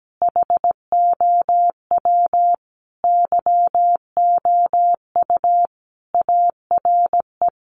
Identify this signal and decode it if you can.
Morse code, heard as a keyed tone, about 17 wpm, 700 Hz. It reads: HOW YOU ARE
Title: Sentences from Top 1400 Words 17wpm